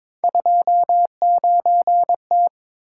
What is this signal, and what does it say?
Morse code, heard as a keyed tone, about 23 wpm, 700 Hz. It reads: 29T
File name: practice / 22wpm / Three Letter-Number Combinations - 22wpm